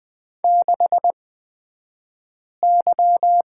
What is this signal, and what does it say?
Morse code, heard as a keyed tone, about 20 wpm, spaced as for 5 wpm, 700 Hz. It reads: 6Y
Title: 254 Lesson 41d - Period - Two Characters 20wpm